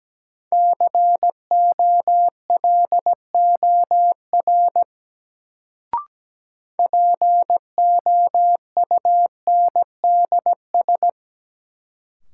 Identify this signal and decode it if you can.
Morse code, heard as a keyed tone, about 17 wpm, 700 Hz. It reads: COLOR POUNDS
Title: Sentences from Top 1000 Words - Review Part 2 - New Word Drill - 17wpm